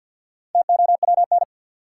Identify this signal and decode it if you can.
Morse code, heard as a keyed tone, about 48 wpm, 700 Hz. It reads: TOWN
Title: Four Letter Words - 50wpm